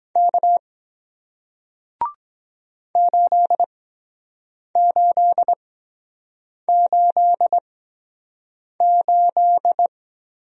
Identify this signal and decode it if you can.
Morse code, heard as a keyed tone, about 23 wpm, spaced as for 7 wpm, 700 Hz. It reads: K 8888